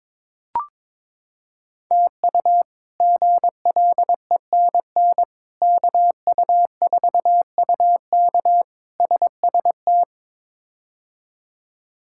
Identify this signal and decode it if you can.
Morse code, heard as a keyed tone, about 22 wpm, 700 Hz. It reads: TU GLENN KU4UK SST
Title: K1USN SST QSO Practice - 22wpm